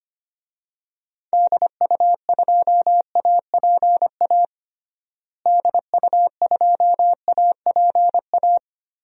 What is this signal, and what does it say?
Morse code, heard as a keyed tone, about 25 wpm, 700 Hz. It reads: DU2APA DU2APA
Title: DXCC-100 Call Signs - 6 Characters - Repeated 3x - 25wpm